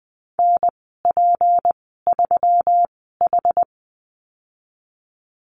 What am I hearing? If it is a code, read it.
Morse code, heard as a keyed tone, about 20 wpm, spaced as for 15 wpm, 700 Hz. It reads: NP3H
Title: 167 Lesson 27g - Number 7 - Call Signs 20wpm